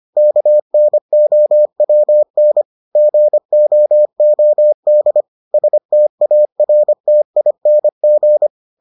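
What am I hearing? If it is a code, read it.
Morse code, heard as a keyed tone, about 25 wpm, 600 Hz. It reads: KNOWN GOOD STARTING
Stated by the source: Morse edition of the Foundations of Amateur Radio episodes.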